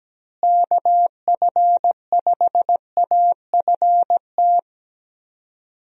Morse code, heard as a keyed tone, about 17 wpm, 700 Hz. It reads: KF5AFT
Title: USA Call Signs - 6 Characters - Repeated 3x - 17wpm